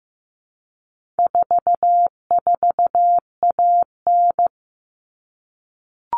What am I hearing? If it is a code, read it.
Morse code, heard as a keyed tone, about 15 wpm, 700 Hz. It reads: 44AN
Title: All Cut Numbers - 4-Digit 15wpm